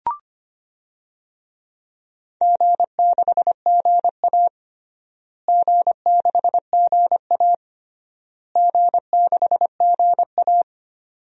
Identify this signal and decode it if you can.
Morse code, heard as a keyed tone, about 25 wpm, 700 Hz. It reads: G6GA G6GA G6GA
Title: DXCC-100 Call Signs - 4 Characters - Repeated 3x - 25wpm